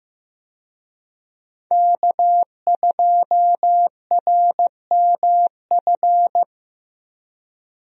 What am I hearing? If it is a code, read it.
Morse code, heard as a keyed tone, about 15 wpm, 700 Hz. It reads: K2RMF
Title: USA Call Signs - 5 Characters - Repeated 2x - 15wpm